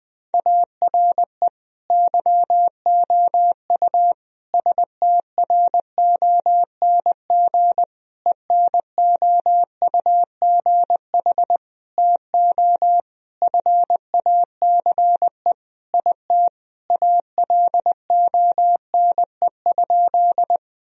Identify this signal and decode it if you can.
Morse code, heard as a keyed tone, about 20 wpm, 700 Hz. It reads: ARE YOU STRONG ENOUGH TO FACE IT ALONE?
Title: Sentences from Top 1000 Words - Review Part 3 - 20wpm